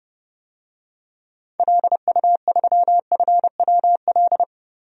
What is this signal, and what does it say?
Morse code, heard as a keyed tone, about 32 wpm, 700 Hz. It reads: LU3FWL
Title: DXCC-100 Call Signs - 6 Characters - 30wpm